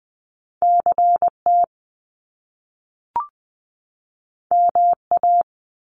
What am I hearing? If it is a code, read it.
Morse code, heard as a keyed tone, about 20 wpm, 700 Hz. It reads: CT MA
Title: US States 20wpm